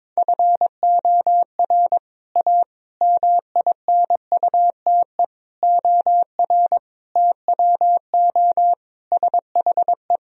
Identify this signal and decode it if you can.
Morse code, heard as a keyed tone, about 22 wpm, 700 Hz. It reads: FOR A MINUTE OR TWO SHE
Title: Sentences from Top 1000 Words - Review Part 1 - 22wpm